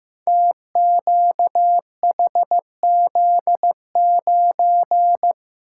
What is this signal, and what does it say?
Morse code, heard as a keyed tone, about 15 wpm, 700 Hz. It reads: TQHZ9
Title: Code Groups - 5 Letter:Numbers - 15wpm